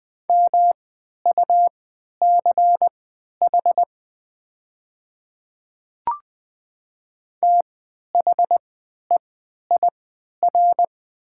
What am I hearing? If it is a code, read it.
Morse code, heard as a keyed tone, about 20 wpm, spaced as for 11 wpm, 700 Hz. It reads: MUCH THEIR
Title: 3x Character Spacing - Top 200 Words - 20wpm